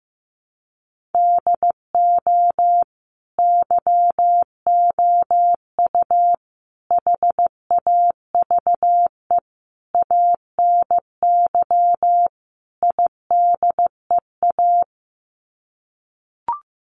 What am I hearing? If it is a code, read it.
Morse code, heard as a keyed tone, about 15 wpm, 700 Hz. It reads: DO YOU HAVE ANY IDEA
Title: Sets of 5 Words Encore 15wpm